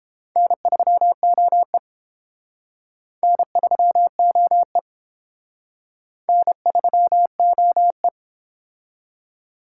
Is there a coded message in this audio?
Yes: Morse code from 0.4 to 8.1 s, heard as a keyed tone, about 30 wpm, 700 Hz. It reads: N3OE N3OE N3OE